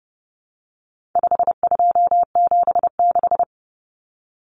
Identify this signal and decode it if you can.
Morse code, heard as a keyed tone, about 30 wpm, 700 Hz. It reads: 5276
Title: Numbers - 4-Digit 30wpm